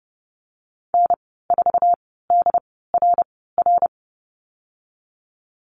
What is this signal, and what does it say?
Morse code, heard as a keyed tone, about 30 wpm, spaced as for 17 wpm, 700 Hz. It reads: N4DRR
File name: Lesson 15g - Number 5 - Call Signs 30wpm